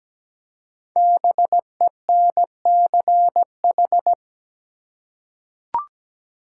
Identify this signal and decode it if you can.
Morse code, heard as a keyed tone, about 17 wpm, 700 Hz. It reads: BENCH
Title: Five Letter Words - 17wpm